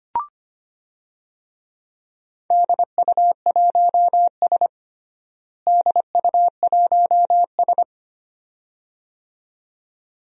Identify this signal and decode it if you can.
Morse code, heard as a keyed tone, about 25 wpm, 700 Hz. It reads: DU1S DU1S
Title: DXCC-100 Call Signs - 4 Characters - Repeated 2x - 25wpm